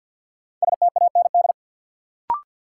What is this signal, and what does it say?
Morse code, heard as a keyed tone, about 53 wpm, 700 Hz. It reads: STAND
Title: Sentences from Top 300 Words - New Word Drill - 50wpm